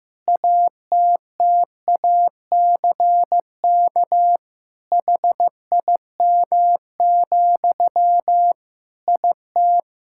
Morse code, heard as a keyed tone, about 15 wpm, 700 Hz. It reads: ATTACK HIM, IT
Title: Sentences from Top 1600 Words 15wpm